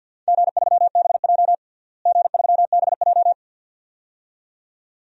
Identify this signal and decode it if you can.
Morse code, heard as a keyed tone, about 46 wpm, 700 Hz. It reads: G3BJ G3BJ
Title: DXCC-100 Call Signs - 4 Characters - Repeated 2x - 50wpm